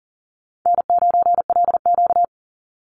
Morse code, heard as a keyed tone, about 40 wpm, 700 Hz. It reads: N9LQ
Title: Easy Call Signs Repeated 2x - 40wpm